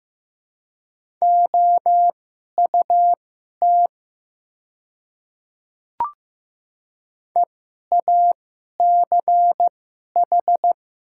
Morse code, heard as a keyed tone, about 15 wpm, spaced as for 11 wpm, 700 Hz. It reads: OUT EACH
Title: 2x Character Spacing - Top 100 Words - 15wpm